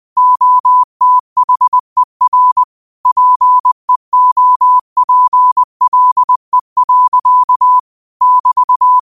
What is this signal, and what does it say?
Morse code, heard as a keyed tone, about 20 wpm, 1000 Hz. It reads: OTHER PEOPLE. <BT>
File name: qotdfiles / 20wpm